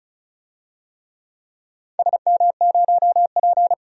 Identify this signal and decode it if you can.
Morse code, heard as a keyed tone, about 36 wpm, 700 Hz. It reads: SM0P
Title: DXCC-100 Call Signs - 4 Characters - 35wpm